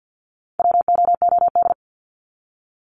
Morse code, heard as a keyed tone, about 52 wpm, 700 Hz. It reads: WOOD
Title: Top 300 Words 50wpm